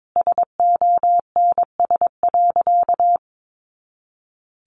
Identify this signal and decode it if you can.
Morse code, heard as a keyed tone, about 22 wpm, 700 Hz. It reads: SONS.
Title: A Sound Of Thunder 22wpm